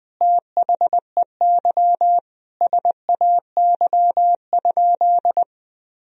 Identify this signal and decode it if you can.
Morse code, heard as a keyed tone, about 20 wpm, 700 Hz. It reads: THEY SAY?
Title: Sentences from Top 1000 Words - Review Part 1 - 20wpm